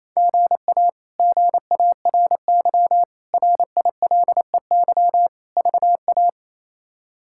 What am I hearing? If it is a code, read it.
Morse code, heard as a keyed tone, about 28 wpm, 700 Hz. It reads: GA GARY RILEY VA
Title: K1USN SST QSO Practice - 28wpm